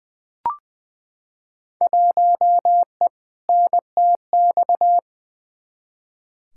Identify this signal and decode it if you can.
Morse code, heard as a keyed tone, about 20 wpm, 700 Hz. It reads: 1E NTX
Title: Field Day Exchanges - 20wpm